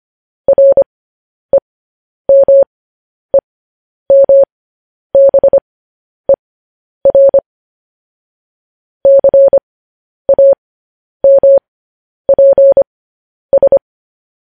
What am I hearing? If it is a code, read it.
Morse code, heard as a keyed tone, about 25 wpm, spaced as for 10 wpm, 550 Hz. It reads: REMEMBER CAMPS
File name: podcast / News headlines-1x-25x10